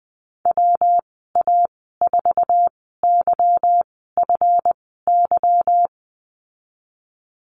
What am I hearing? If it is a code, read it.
Morse code, heard as a keyed tone, about 20 wpm, spaced as for 15 wpm, 700 Hz. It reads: WA4YFY
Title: 161 Lesson 26g - Letter g - Call Signs 20wpm